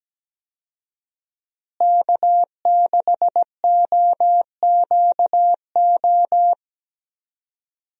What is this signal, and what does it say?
Morse code, heard as a keyed tone, about 17 wpm, 700 Hz. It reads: K6OQO
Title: USA Call Signs - 5 Characters - Repeated 3x - 17wpm